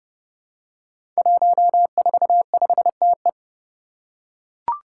Morse code, heard as a keyed tone, about 28 wpm, 700 Hz. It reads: 145TE
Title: Five Challenging Letter-Number Combinations 30wpm